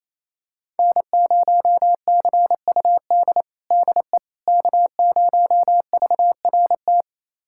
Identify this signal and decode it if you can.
Morse code, heard as a keyed tone, about 28 wpm, 700 Hz. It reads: N0CUD DE K0VRT
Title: USA Callsign DE Callsign - 5 Characters - 28wpm